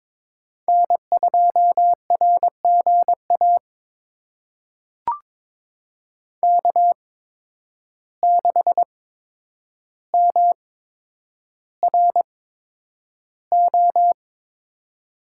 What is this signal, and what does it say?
Morse code, heard as a keyed tone, about 22 wpm, spaced as for 9 wpm, 700 Hz. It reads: N2RGA K6MRO
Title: USA Call Signs - 8x Character Spacing - 22wpm